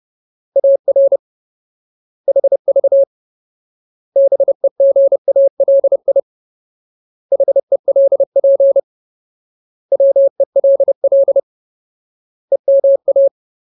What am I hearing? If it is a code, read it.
Morse code, heard as a keyed tone, about 30 wpm, 550 Hz. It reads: AR HV BEGALI HELP WELL EMA
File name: Ultimate Word List / 30WPM Ultimate